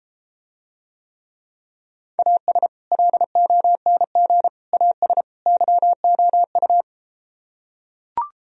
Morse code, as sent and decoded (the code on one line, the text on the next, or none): .- ... / .-.. --- -. --. / .- ... / -.-- --- ..-
AS LONG AS YOU